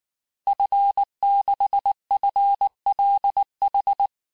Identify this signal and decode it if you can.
Morse code, heard as a keyed tone, about 19 wpm, 800 Hz. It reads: F6FLH
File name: cours-du-lundi files / cours-1595--FIN-de-QSO-vitesse-18mpm